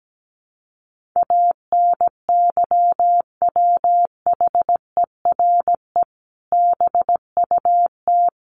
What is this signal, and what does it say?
Morse code, heard as a keyed tone, about 17 wpm, 700 Hz. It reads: ANYWHERE BUT
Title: Sentences from Top 1400 Words 17wpm